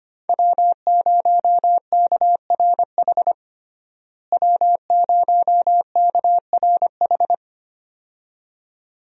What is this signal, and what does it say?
Morse code, heard as a keyed tone, about 25 wpm, 700 Hz. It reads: W0KRH W0KRH
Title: USA Call Signs - 5 Characters - Repeated 2x - 25wpm